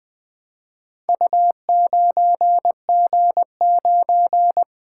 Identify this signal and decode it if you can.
Morse code, heard as a keyed tone, about 20 wpm, 700 Hz. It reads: U9G9
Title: Four Letter-Number Combinations - 20wpm